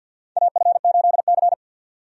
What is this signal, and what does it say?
Morse code, heard as a keyed tone, about 53 wpm, 700 Hz. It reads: AF8C
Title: USA Call Signs - 4 Characters - 50wpm